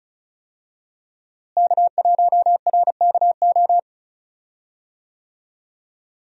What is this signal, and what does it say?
Morse code, heard as a keyed tone, about 36 wpm, 700 Hz. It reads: K1RKO
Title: USA Call Signs - 5 Characters - Repeated 2x - 35wpm